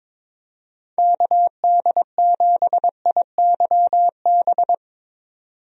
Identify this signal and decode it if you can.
Morse code, heard as a keyed tone, about 22 wpm, 700 Hz. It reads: KD7IYB
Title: USA Call Signs - 6 Characters - 22wpm